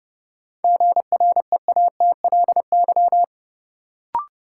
Morse code, heard as a keyed tone, about 30 wpm, 700 Hz. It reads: GREATLY
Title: Sentences from Top 1400 Words - New Word Drill Alt - 30wpm